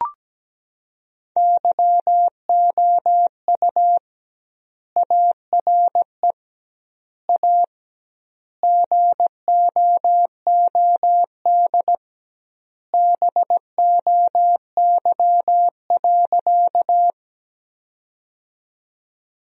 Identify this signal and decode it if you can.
Morse code, heard as a keyed tone, about 17 wpm, 700 Hz. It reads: YOU ARE A GOOD BOY.
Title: 2x Word-Spacing - Sentences from Top 200 Words - 17wpm